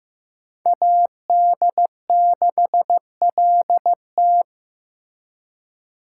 Morse code, heard as a keyed tone, about 18 wpm, 700 Hz. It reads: AD6LT
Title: Speed Racing - USA 2x2 Call Signs 15wpm